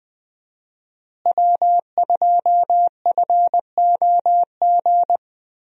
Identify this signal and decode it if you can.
Morse code, heard as a keyed tone, about 20 wpm, 700 Hz. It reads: W2FOG
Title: USA Call Signs - 5 Characters - Repeated 3x - 20wpm